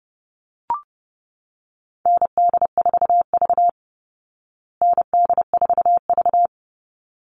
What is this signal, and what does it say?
Morse code, heard as a keyed tone, about 30 wpm, 700 Hz. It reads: ND4V ND4V
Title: Call Signs Repeated 3x - 30wpm